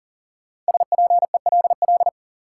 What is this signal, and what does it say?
Morse code, heard as a keyed tone, about 40 wpm, 700 Hz. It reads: SPELL
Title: Sentences from Top 1000 Words - Review Part 1 - New Word Drill - 40wpm